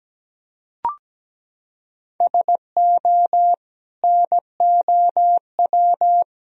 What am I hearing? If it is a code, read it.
Morse code, heard as a keyed tone, about 17 wpm, 700 Hz. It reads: SO NOW
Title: Sentences from Top 1000 Words - Review Part 1 - 17wpm